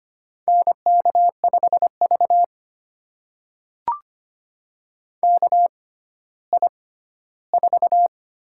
Morse code, heard as a keyed tone, about 25 wpm, spaced as for 14 wpm, 700 Hz. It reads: NK5V KI4
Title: USA Call Signs - 4 Characters - 6x Character Spacing - 25wpm